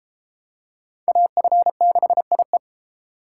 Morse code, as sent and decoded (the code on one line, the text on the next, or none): .- ..-. -.... .. .
AF6IE